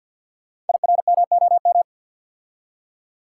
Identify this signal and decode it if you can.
Morse code, heard as a keyed tone, about 50 wpm, 690 Hz. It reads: IRMOM
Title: Code Groups - 5 Letters - 50wpm